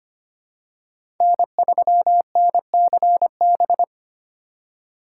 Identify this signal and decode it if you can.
Morse code, heard as a keyed tone, about 25 wpm, 700 Hz. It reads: N3NCB
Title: USA Call Signs - 4x Character Spacing - 25wpm